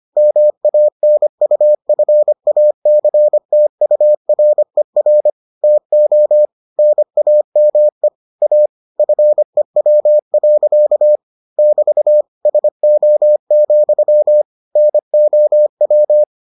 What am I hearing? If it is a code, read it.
Morse code, heard as a keyed tone, about 25 wpm, 600 Hz. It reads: MANUFACTURER TO NAME A FEW. <BT> SO, NOW